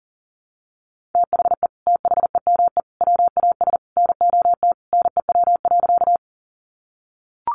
Morse code, heard as a keyed tone, about 40 wpm, 700 Hz. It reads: THE THEME WAS NOT NEW.
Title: Sentences from Top 1100 Words 40wpm